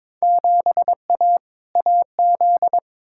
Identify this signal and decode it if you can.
Morse code, heard as a keyed tone, about 22 wpm, 700 Hz. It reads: 7A AZ
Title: Field Day Exchanges - 22wpm